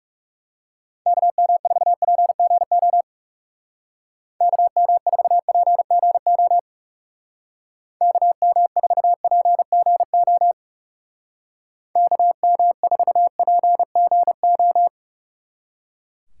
Morse code, heard as a keyed tone, about 36 wpm, 700 Hz. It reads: KM4PGO KM4PGO KM4PGO KM4PGO
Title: Speed Racing - USA 2x3 Call Signs 30wpm